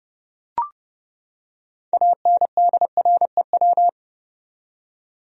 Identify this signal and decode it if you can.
Morse code, heard as a keyed tone, about 30 wpm, 700 Hz. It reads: ANDREW
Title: USA Common Names - 30wpm